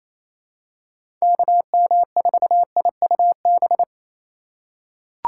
Morse code, heard as a keyed tone, about 28 wpm, 700 Hz. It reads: KM4IUB
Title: USA Call Signs - 6 Characters - 3x Character Spacing - 28wpm